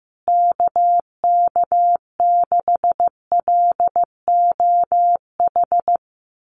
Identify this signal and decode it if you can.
Morse code, heard as a keyed tone, about 14 wpm, 700 Hz. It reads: KK6LOH
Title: USA Call Signs Repeated 2x 15wpm